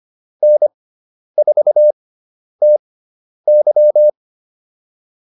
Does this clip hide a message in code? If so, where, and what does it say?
Yes: Morse code from 0.4 to 4.1 s, heard as a keyed tone, about 25 wpm, spaced as for 10 wpm, 600 Hz. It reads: N4TY